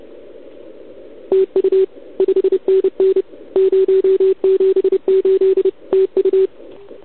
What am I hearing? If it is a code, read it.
Morse code, heard as a keyed tone, about 29 wpm, 380 Hz. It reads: TU 5NN 078 TU
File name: ARCHIV / OKDX2019 / OKOM40 KX3